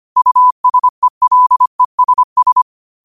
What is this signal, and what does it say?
Morse code, heard as a keyed tone, about 25 wpm, 1000 Hz. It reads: USELESS
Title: 25 WPM morse code quotes for Thu, 29 Feb 2024